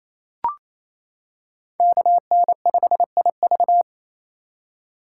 Morse code, heard as a keyed tone, about 28 wpm, 700 Hz. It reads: KN5IV
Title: Code Groups - 5 Letter:Numbers - 28wpm